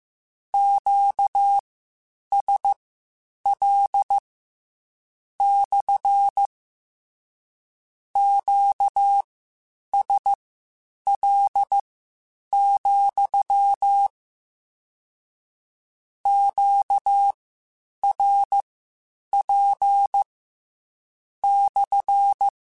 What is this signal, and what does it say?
Morse code, heard as a keyed tone, about 15 wpm, spaced as for 8 wpm, 800 Hz. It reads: QSL / QSL, QRP /